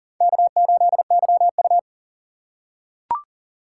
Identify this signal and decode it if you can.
Morse code, heard as a keyed tone, about 37 wpm, 700 Hz. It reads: K8YU
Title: USA Call Signs - 4 Characters - 6x Character Spacing - 40wpm